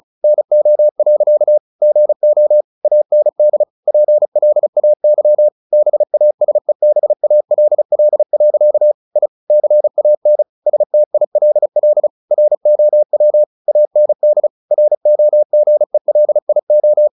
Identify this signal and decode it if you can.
Morse code, heard as a keyed tone, about 34 wpm, 600 Hz. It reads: NO. GO AND PLAY BASEBALL. I CAN STILL ROW AND ROGELIO
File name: the old man and the sea / the old man and the sea 1 35wpm